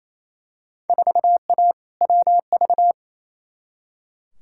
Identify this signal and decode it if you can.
Morse code, heard as a keyed tone, about 28 wpm, 700 Hz. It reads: 4A WV
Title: Field Day Exchanges Encore - 28wpm